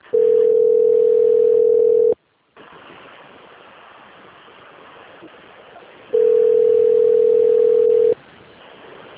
Ringback tone (North America), 440 + 480 Hz.